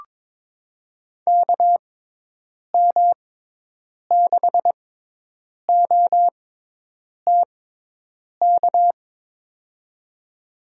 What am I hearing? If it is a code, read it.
Morse code, heard as a keyed tone, about 22 wpm, spaced as for 8 wpm, 700 Hz. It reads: KM6OTK
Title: USA Call Signs - 6 Characters - 6x Character Spacing - 22wpm